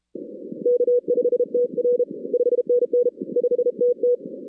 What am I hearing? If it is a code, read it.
Morse code, heard as a keyed tone, about 35 wpm, 530 Hz. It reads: K5TR 5NN 5TT
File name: 2016 arrl dx cw / 2016-ARRL-DX-CW-K5TR- - 0-41